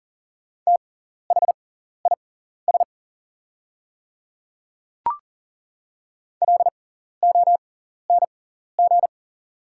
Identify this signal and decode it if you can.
Morse code, heard as a keyed tone, about 40 wpm, spaced as for 14 wpm, 700 Hz. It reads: THIS LONG
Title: 6x Character Spacing - Top 100 Words - 40wpm